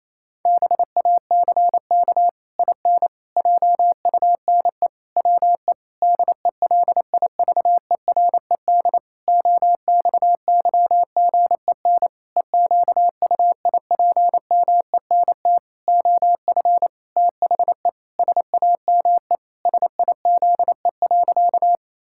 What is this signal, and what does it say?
Morse code, heard as a keyed tone, about 28 wpm, 700 Hz. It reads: BACK IN JUNE WE DELIVERED OXYGEN EQUIPMENT OF THE SAME SIZE.
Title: Pangram Challenge v2 - 28wpm